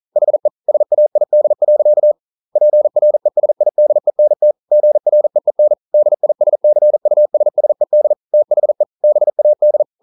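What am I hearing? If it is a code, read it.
Morse code, heard as a keyed tone, about 42 wpm, 600 Hz. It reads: HE SAID. PRESIDENT GREEN DISCUSSED THE BAD
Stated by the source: Morse code practice at 41 wpm, from CW med Gnister.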